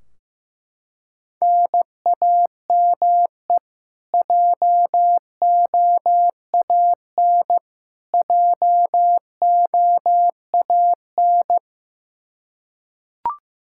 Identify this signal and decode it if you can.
Morse code, heard as a keyed tone, about 16 wpm, 700 Hz. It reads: NAME JOAN JOAN
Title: Practice QSO - Exchange 1 - Beginning - 15wpm